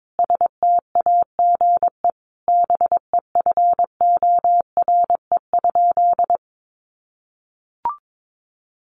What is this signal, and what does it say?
Morse code, heard as a keyed tone, about 22 wpm, 700 Hz. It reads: STAGE BEFORE?
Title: Sentences from Top 1200 Words 22wpm